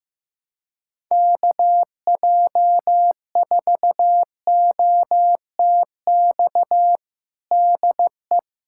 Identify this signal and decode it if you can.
Morse code, heard as a keyed tone, about 16 wpm, 700 Hz. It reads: KJ4OTX DE
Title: USA Callsign DE Callsign - 6 Characters - 15wpm